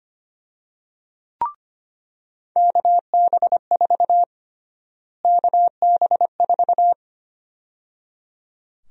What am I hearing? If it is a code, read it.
Morse code, heard as a keyed tone, about 25 wpm, 700 Hz. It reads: KB4 KB4
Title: USA Call Sign Prefixes - Repeated 2x - 25wpm